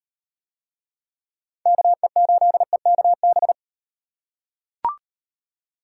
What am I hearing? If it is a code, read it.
Morse code, heard as a keyed tone, about 38 wpm, 700 Hz. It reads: KE8EKB